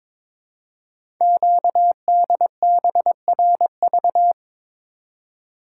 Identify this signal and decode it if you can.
Morse code, heard as a keyed tone, about 22 wpm, 700 Hz. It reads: QDBRV
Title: Code Groups - 5 Letters - 22wpm